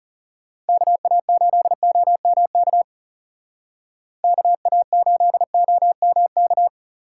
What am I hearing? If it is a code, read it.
Morse code, heard as a keyed tone, about 39 wpm, 700 Hz. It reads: KA8OMK KA8OMK